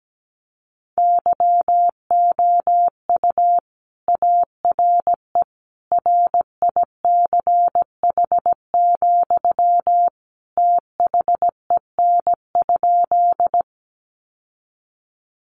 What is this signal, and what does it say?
Morse code, heard as a keyed tone, about 17 wpm, 700 Hz. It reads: YOU ARE RICH, THEN?
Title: Sentences from Top 1100 Words 17wpm